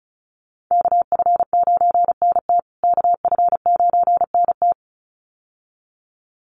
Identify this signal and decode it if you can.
Morse code, heard as a keyed tone, about 34 wpm, 700 Hz. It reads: KF9NT KF9NT
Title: USA Call Signs Repeated 2x 35wpm